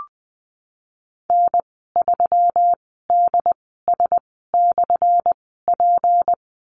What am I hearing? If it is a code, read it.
Morse code, heard as a keyed tone, about 20 wpm, spaced as for 15 wpm, 700 Hz. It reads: N3DS/P
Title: 179 Lesson 29g - Slash - Call Signs 20wpm